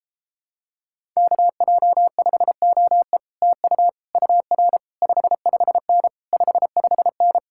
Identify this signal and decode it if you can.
Morse code, heard as a keyed tone, about 29 wpm, 700 Hz. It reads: KJ5OE TU UR 55N 55N